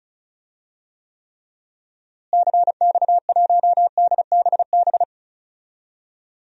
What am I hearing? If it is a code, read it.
Morse code, heard as a keyed tone, about 34 wpm, 700 Hz. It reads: CX1DBB
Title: DXCC-100 Call Signs - 6 Characters - Repeated 2x - 35wpm